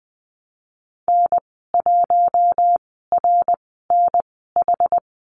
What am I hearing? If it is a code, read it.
Morse code, heard as a keyed tone, about 20 wpm, spaced as for 15 wpm, 700 Hz. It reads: N1RNH
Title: 088 Lesson 15g - Number 5 - Call Signs 20wpm